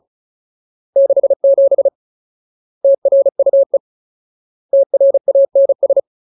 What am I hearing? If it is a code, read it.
Morse code, heard as a keyed tone, about 34 wpm, 550 Hz. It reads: 67 TRUE TRANS
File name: Ultimate Word List / 35WPM Ultimate